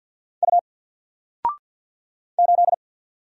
Morse code, heard as a keyed tone, about 48 wpm, 700 Hz. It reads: U 8
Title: Single Letter-Number Alt 50wpm